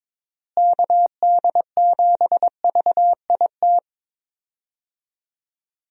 Morse code, heard as a keyed tone, about 22 wpm, 700 Hz. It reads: KD7VIT
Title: USA Call Signs - 6 Characters - Repeated 2x - 22wpm